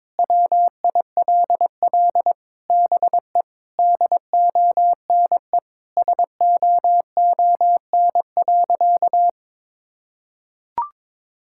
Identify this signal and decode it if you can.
Morse code, heard as a keyed tone, about 22 wpm, 700 Hz. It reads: WILL BE DONE SOON.
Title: Sentences from Top 1600 Words 22wpm